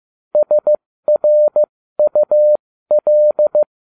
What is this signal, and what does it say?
Morse code, heard as a keyed tone, about 15 wpm, spaced as for 13 wpm, 600 Hz. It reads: SRUL